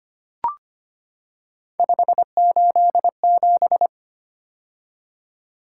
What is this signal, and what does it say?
Morse code, heard as a keyed tone, about 25 wpm, 700 Hz. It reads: 587
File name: Three Letter-Number Combinations - 25wpm